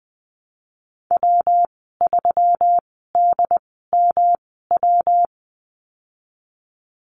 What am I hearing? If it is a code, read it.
Morse code, heard as a keyed tone, about 20 wpm, spaced as for 15 wpm, 700 Hz. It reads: W3DMW
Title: 219 Lesson 35g - Number 0 - Call Signs 20wpm